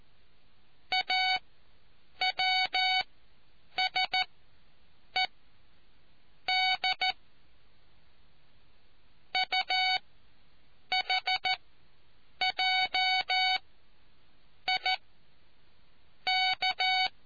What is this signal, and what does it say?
Morse code, heard as a keyed tone, about 14 wpm, spaced as for 7 wpm, 750 Hz. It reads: AWSED UHJI K